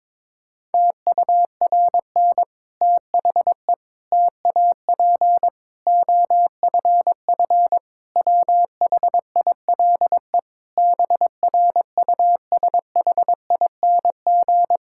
Morse code, heard as a keyed tone, about 22 wpm, 700 Hz. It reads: TURN THE TAP OFF WHILE BRUSHING
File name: Difficult Sentences with Hints - 22wpm